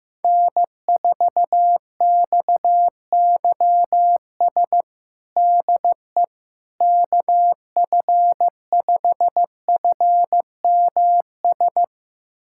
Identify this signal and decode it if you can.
Morse code, heard as a keyed tone, about 15 wpm, 700 Hz. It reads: N4XYS DE KF5FMS
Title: USA Callsign DE Callsign 15wpm